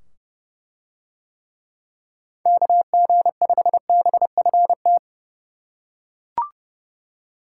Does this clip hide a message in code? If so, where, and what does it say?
Yes: Morse code from 2.5 to 5.0 s, heard as a keyed tone, about 28 wpm, 700 Hz. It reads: KG5BFT